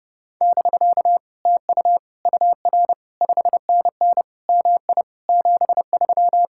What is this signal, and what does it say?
Morse code, heard as a keyed tone, about 29 wpm, 700 Hz. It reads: <BK> TU UR 5NN MI 73